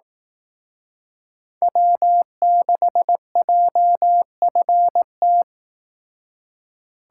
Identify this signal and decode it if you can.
Morse code, heard as a keyed tone, about 18 wpm, 700 Hz. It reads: W6JFT
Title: Speed Racing - USA 1x3 Call Signs 15wpm